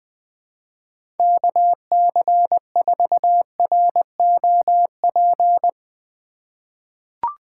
Morse code, heard as a keyed tone, about 20 wpm, 700 Hz. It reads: KC4ROP
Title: USA Call Signs - 3x Character Spacing - 20wpm